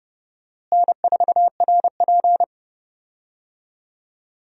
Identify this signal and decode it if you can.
Morse code, heard as a keyed tone, about 30 wpm, 700 Hz. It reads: N4RP
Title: USA Call Signs - 4 Characters - Repeated 3x - 30wpm